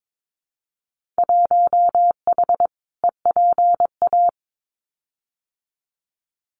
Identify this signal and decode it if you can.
Morse code, heard as a keyed tone, about 22 wpm, 700 Hz. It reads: 1H EPA
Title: Winter Field Day Exchanges 22wpm